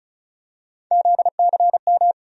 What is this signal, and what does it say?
Morse code, heard as a keyed tone, about 34 wpm, 700 Hz. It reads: ZCM
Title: USA Call Sign Postfixes - 3 Letters 35wpm